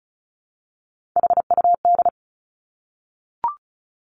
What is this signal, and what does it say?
Morse code, heard as a keyed tone, about 36 wpm, 700 Hz. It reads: HUD
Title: Three Challenging Letter-Number Combinations 35wpm